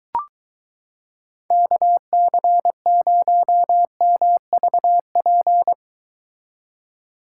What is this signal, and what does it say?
Morse code, heard as a keyed tone, about 22 wpm, 700 Hz. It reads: KC0MVP